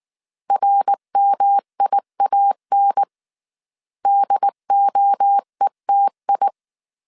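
Morse code, heard as a keyed tone, about 19 wpm, 800 Hz. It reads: RMIAN DOETI
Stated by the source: Morse code dictation at 18 wpm from the IW3SGT QRS Morse code training course, five-character groups of 300 characters.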